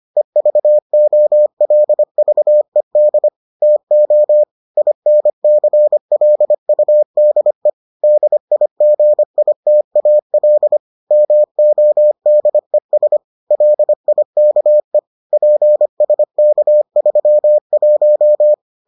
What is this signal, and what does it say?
Morse code, heard as a keyed tone, about 25 wpm, 600 Hz. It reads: EVOLVED TO INCLUDE DIGITAL MODES LIKE PSK31